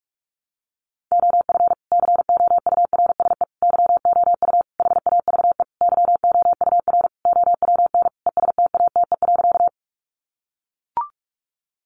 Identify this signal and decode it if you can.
Morse code, heard as a keyed tone, about 44 wpm, 700 Hz. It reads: OF COURSE YOU HAVE YOUR OWN ESTATE.
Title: Sentences from Top 1200 Words 45wpm